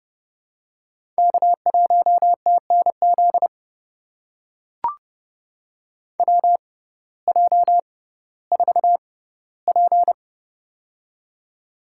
Morse code, heard as a keyed tone, about 30 wpm, spaced as for 17 wpm, 700 Hz. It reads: K1TNZ WJ4P
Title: USA Call Signs - 6x Character Spacing - 30wpm